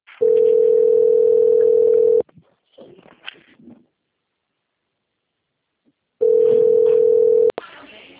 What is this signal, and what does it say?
Ringback tone (North America), 440 + 480 Hz.